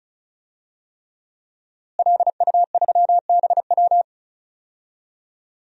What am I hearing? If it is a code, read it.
Morse code, heard as a keyed tone, about 36 wpm, 700 Hz. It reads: LU3BW
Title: DXCC-100 Call Signs - 5 Characters - Repeated 3x - 35wpm